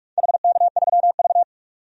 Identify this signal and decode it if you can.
Morse code, heard as a keyed tone, about 45 wpm, 700 Hz. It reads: HK3V
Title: DXCC-100 Call Signs - 4 Characters - 45wpm